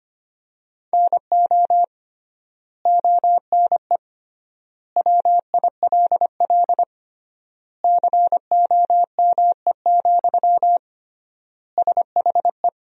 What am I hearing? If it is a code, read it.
Morse code, heard as a keyed tone, about 25 wpm, 700 Hz. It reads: NO ONE WILL COME, SHE
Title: 3x Word-Spacing - Sentences from Top 100 Words - 25wpm